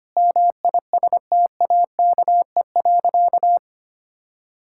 Morse code, heard as a keyed tone, about 25 wpm, 700 Hz. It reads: MISTAKE.
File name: practice / 25wpm / Sentences with 3-Word Phrases - 25wpm